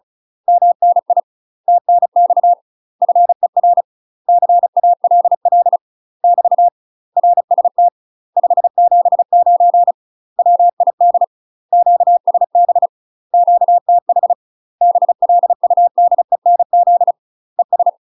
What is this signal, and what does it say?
Morse code, heard as a keyed tone, about 36 wpm, 700 Hz. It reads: MNI TNX FER CALL <BT> RST 579 WID QSB QTH BLUDENZ ES
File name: UK MASTER QSOs TRAINING / 35WPM / UK MASTER QSOs 0022